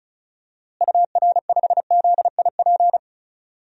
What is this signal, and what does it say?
Morse code, heard as a keyed tone, about 36 wpm, 700 Hz. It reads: UR5ZIP
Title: DXCC-100 Call Signs - 6 Characters - Repeated 3x - 35wpm